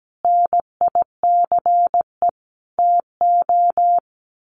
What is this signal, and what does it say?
Morse code, heard as a keyed tone, about 17 wpm, 700 Hz. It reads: NICE TO
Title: Sets of 5 Words Encore2 17wpm